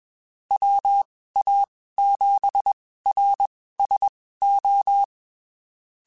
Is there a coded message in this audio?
Yes: Morse code from 0.5 to 5.0 s, heard as a keyed tone, about 21 wpm, spaced as for 15 wpm, 800 Hz. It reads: WA7RSO